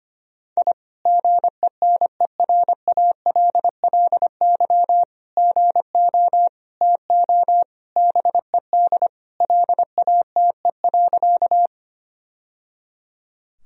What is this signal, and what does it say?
Morse code, heard as a keyed tone, about 25 wpm, 700 Hz. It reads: I GENERALLY GO TO BED LATE.
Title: Sentences from Top 1000 Words - Review Part 4 - 25wpm